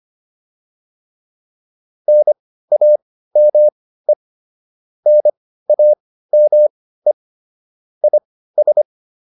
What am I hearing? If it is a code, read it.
Morse code, heard as a keyed tone, about 25 wpm, spaced as for 15 wpm, 600 Hz. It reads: NAME NAME IS